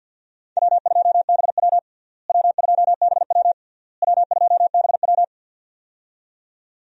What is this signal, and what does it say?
Morse code, heard as a keyed tone, about 52 wpm, 700 Hz. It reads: W2BW W2BW W2BW
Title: USA Call Signs - 4 Characters - Repeated 3x - 50wpm